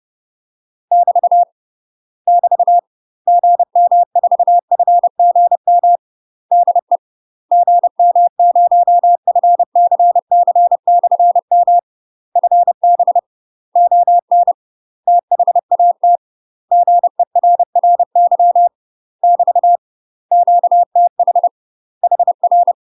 Morse code, heard as a keyed tone, about 30 wpm, 700 Hz. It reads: <BT> <BT> GM4FGM DE GM0FCC/M FB ON THAT GERRY <BT> QTH HR
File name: UK MASTER QSOs TRAINING / 30WPM / UK MASTER QSOs 0008